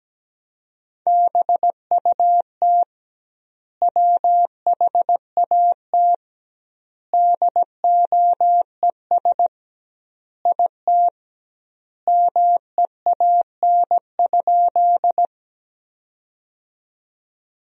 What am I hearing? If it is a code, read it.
Morse code, heard as a keyed tone, about 17 wpm, 700 Hz. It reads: BUT WHAT DOES IT MEAN?
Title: 2x Word-Spacing - Sentences from Top 200 Words - 17wpm